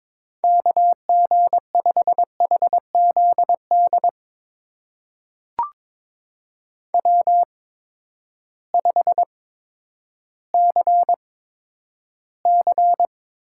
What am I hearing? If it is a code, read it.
Morse code, heard as a keyed tone, about 22 wpm, spaced as for 11 wpm, 700 Hz. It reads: KG5HZD W5CC
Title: USA Call Signs - 8x Character Spacing - 22wpm